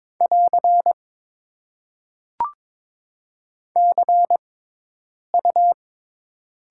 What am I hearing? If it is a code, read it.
Morse code, heard as a keyed tone, about 22 wpm, spaced as for 8 wpm, 700 Hz. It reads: <AR> CU
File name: Top 90 QSO Elements - 6x Character Spacing - 22wpm